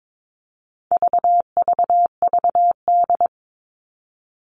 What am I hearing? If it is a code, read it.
Morse code, heard as a keyed tone, about 22 wpm, 700 Hz. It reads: VVVD
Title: All Cut Numbers - 4-Digit 22wpm